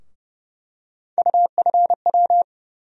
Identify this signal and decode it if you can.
Morse code, heard as a keyed tone, about 30 wpm, 700 Hz. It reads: UFW
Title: USA Call Sign Postfixes - 3 Letters 30wpm